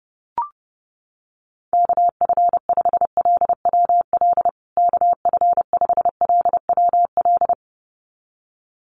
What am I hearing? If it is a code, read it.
Morse code, heard as a keyed tone, about 30 wpm, 700 Hz. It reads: KF5LWL KF5LWL
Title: USA Call Signs Repeated 2x 30wpm